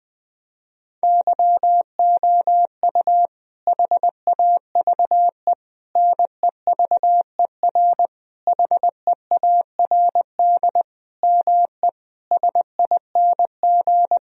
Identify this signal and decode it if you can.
Morse code, heard as a keyed tone, about 20 wpm, 700 Hz. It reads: YOU HAVE NEVER HEARD ME SING
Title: Sentences from Top 1000 Words - Review Part 2 - 20wpm